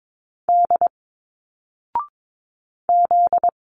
Morse code, heard as a keyed tone, about 22 wpm, 700 Hz. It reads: D Z
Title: Single Letters 22wpm